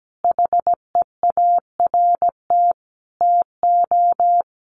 Morse code, heard as a keyed tone, about 17 wpm, 700 Hz. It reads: HEART TO
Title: Sentences from Top 1500 Words 17wpm